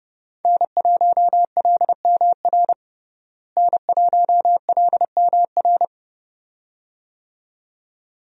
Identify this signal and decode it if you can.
Morse code, heard as a keyed tone, about 30 wpm, 700 Hz. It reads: N1LMR N1LMR
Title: USA Call Signs - 5 Characters - Repeated 2x - 30wpm